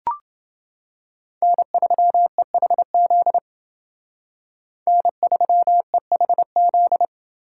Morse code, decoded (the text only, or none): N3EHZ N3EHZ